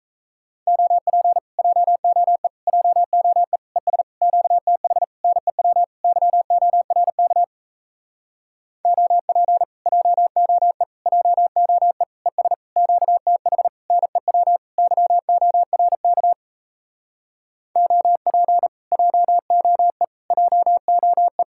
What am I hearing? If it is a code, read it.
Morse code, heard as a keyed tone, about 38 wpm, 700 Hz. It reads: OP JOE JOE ES QTH NEW YORK OP JOE JOE ES QTH NEW YORK OP JOE JOE